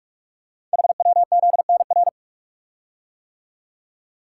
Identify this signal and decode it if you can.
Morse code, heard as a keyed tone, about 44 wpm, 700 Hz. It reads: HWZNR
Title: Code Groups - 5 Letter:Numbers - 45wpm